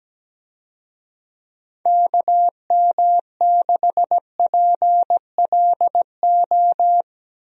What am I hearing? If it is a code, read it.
Morse code, heard as a keyed tone, about 17 wpm, 700 Hz. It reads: KM6PLO
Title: USA Call Signs - 6 Characters - 17wpm